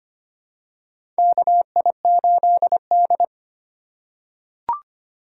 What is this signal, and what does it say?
Morse code, heard as a keyed tone, about 25 wpm, 700 Hz. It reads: KI8D
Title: USA Call Signs - 6x Character Spacing - 25wpm